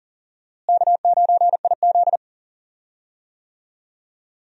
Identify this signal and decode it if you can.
Morse code, heard as a keyed tone, about 40 wpm, 700 Hz. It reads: K9IZ
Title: USA Call Signs - 4 Characters - Repeated 2x - 40wpm